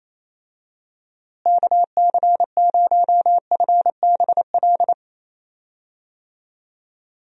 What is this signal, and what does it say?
Morse code, heard as a keyed tone, about 27 wpm, 700 Hz. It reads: KC0FBL
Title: USA Call Signs - 6 Characters - Repeated 2x - 28wpm